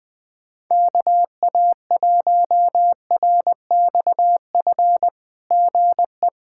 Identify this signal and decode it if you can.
Morse code, heard as a keyed tone, about 20 wpm, 700 Hz. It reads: KA1RXF GE
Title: SOTA QSO Practice 20wpm